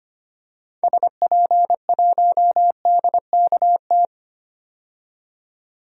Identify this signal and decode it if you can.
Morse code, heard as a keyed tone, about 25 wpm, 700 Hz. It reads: SP1DKT
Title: DXCC-100 Call Signs - 6 Characters - 25wpm